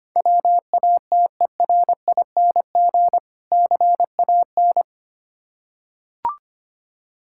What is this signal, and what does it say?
Morse code, heard as a keyed tone, about 25 wpm, 700 Hz. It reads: WATERING CAN
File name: practice / 25wpm / Difficult Words with Hints - 25wpm